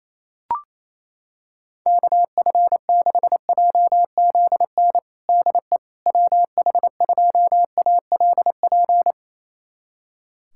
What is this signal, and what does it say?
Morse code, heard as a keyed tone, about 28 wpm, 700 Hz. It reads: KF6JZN DE WH2ALP
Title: USA Callsign DE Callsign - 6 Characters - 28wpm